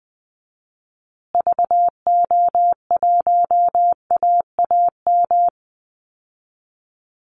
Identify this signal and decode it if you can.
Morse code, heard as a keyed tone, about 20 wpm, 700 Hz. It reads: VO1AAM
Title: Call Signs Repeated 3x - 20wpm